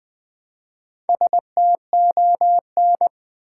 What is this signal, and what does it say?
Morse code, heard as a keyed tone, about 20 wpm, 700 Hz. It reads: STON
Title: Build A Word - 1 Letter At A Time - 5-Letter Words - 20wpm